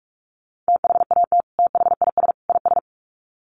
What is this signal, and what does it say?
Morse code, heard as a keyed tone, about 44 wpm, 700 Hz. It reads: THAT THIS IS
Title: Sets of 3 Words 45wpm